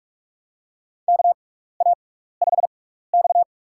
Morse code, heard as a keyed tone, about 46 wpm, spaced as for 16 wpm, 700 Hz. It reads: KA5X
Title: USA Call Signs - 4 Characters - 6x Character Spacing - 45wpm